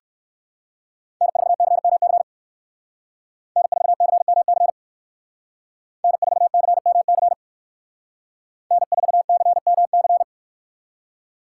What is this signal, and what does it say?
Morse code, heard as a keyed tone, about 58 wpm, 700 Hz. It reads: N4CMC N4CMC N4CMC N4CMC